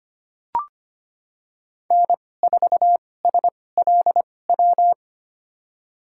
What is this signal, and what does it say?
Morse code, heard as a keyed tone, about 25 wpm, spaced as for 18 wpm, 700 Hz. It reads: N4SLW
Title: USA Call Signs - 2x Character Spacing - 25wpm